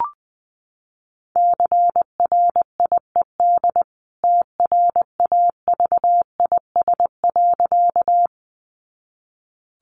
Morse code, heard as a keyed tone, about 20 wpm, 700 Hz. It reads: CRIED TRAVIS.
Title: A Sound Of Thunder 20wpm